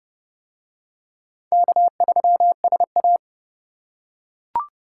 Morse code, heard as a keyed tone, about 28 wpm, 690 Hz. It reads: K3SA